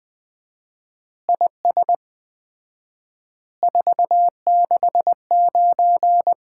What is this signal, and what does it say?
Morse code, heard as a keyed tone, about 20 wpm, 700 Hz. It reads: IS 469
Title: Common QSO Phrases - 4x Word-Spacing - 20wpm